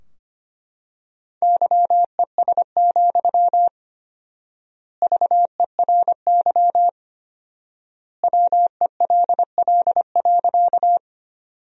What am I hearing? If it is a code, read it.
Morse code, heard as a keyed tone, about 25 wpm, 700 Hz. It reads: YES, VERY WELL.
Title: 4x Word-Spacing - Sentences from Top 500 Words - 25wpm